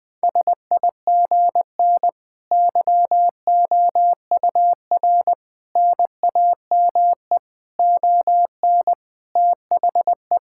Morse code, heard as a keyed tone, about 21 wpm, 700 Hz. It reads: SIGN YOUR NAME ON THE
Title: Sentences with 5-Word Phrases - 20wpm